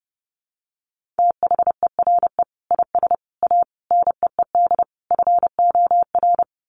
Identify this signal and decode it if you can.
Morse code, heard as a keyed tone, about 30 wpm, 700 Hz. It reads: THERE IS A NEED FOR
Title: Sets of 5 Words Encore2 30wpm